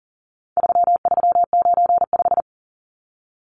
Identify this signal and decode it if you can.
Morse code, heard as a keyed tone, about 40 wpm, 700 Hz. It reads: 3395
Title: Numbers - 4-Digit 40wpm